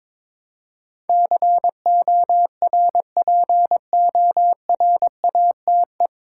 Morse code, heard as a keyed tone, about 22 wpm, 700 Hz. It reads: CORPORATE
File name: 9 Letter Words - First 3 Letters Given - 22wpm